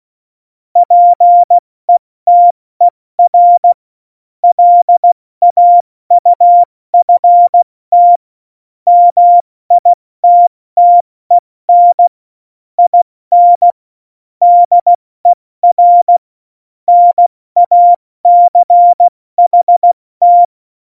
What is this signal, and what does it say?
Morse code, heard as a keyed tone, about 16 wpm, 700 Hz. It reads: PETER LAUFT MITTEN IN DER NACHT
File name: Morsekurs / Witze / Witz 12. Dezember 2017, 70 F80